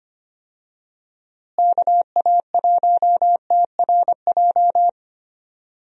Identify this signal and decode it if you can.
Morse code, heard as a keyed tone, about 25 wpm, 700 Hz. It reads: KA1TRJ
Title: USA Call Signs - 6 Characters - Repeated 2x - 25wpm